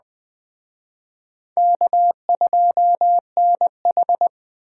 Morse code, heard as a keyed tone, about 23 wpm, 700 Hz. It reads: K2NH